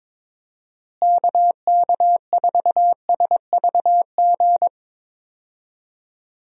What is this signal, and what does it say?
Morse code, heard as a keyed tone, about 22 wpm, 700 Hz. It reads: KK4SVG
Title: USA Call Signs - 6 Characters - Repeated 3x - 22wpm